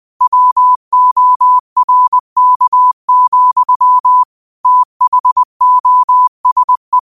Morse code, heard as a keyed tone, about 20 wpm, 1000 Hz. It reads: WORK, THOSE
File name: qotdfiles / 20wpm